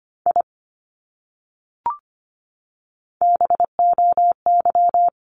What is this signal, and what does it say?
Morse code, heard as a keyed tone, about 25 wpm, 700 Hz. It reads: I BOY
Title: Top 200 Words Repeated 25wpm